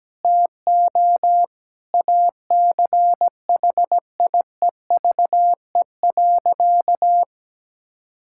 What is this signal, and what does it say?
Morse code, heard as a keyed tone, about 17 wpm, 700 Hz. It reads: TO ACHIEVE.
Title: Sentences from Top 1000 Words - Review Part 6 - 17wpm